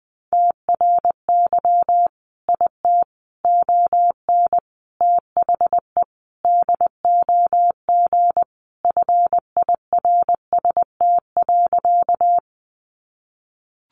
Morse code, heard as a keyed tone, about 20 wpm, 700 Hz. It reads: TRY IT ON THE DOG FIRST.
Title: Sentences from Top 400 Words 20wpm